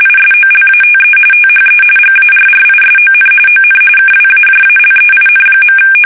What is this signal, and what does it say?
RTTY, about 45.45 baud. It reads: ISOLATE THE GARRISON ON KWAJALEIN.
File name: fsk850 / USS-Massachusetts